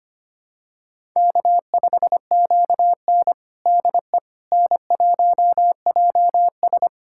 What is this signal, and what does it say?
Morse code, heard as a keyed tone, about 25 wpm, 700 Hz. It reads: K5QN DE N1JS
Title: USA Callsign DE Callsign - 4 Characters - 25wpm